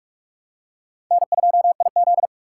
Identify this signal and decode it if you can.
Morse code, heard as a keyed tone, about 41 wpm, 700 Hz. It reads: N2IZ